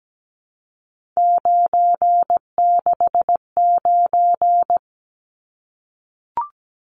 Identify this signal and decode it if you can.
Morse code, heard as a keyed tone, about 17 wpm, 700 Hz. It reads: 969
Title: Numbers - 3-Digit 17wpm